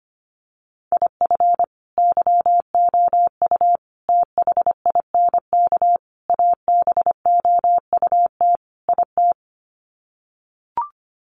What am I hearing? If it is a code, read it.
Morse code, heard as a keyed tone, about 25 wpm, 700 Hz. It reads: IF YOU THINK ABOUT IT
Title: Sets of 5 Words Encore2 25wpm